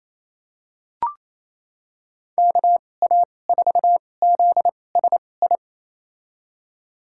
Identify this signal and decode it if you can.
Morse code, heard as a keyed tone, about 27 wpm, spaced as for 20 wpm, 700 Hz. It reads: KA4ZSI
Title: USA Call Signs - 2x Character Spacing - 28wpm